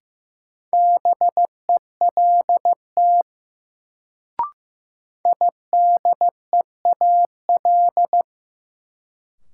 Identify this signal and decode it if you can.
Morse code, heard as a keyed tone, about 15 wpm, 700 Hz. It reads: BELT IDEAL
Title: Sentences from Top 1400 Words - New Word Drill Alt - 15wpm